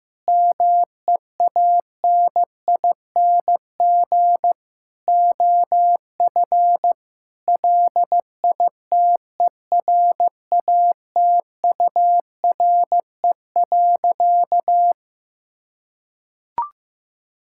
Morse code, heard as a keyed tone, about 15 wpm, 700 Hz. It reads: MEANING OF LITERATURE.
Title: Sentences from Top 1000 Words - Review Part 6 - 15wpm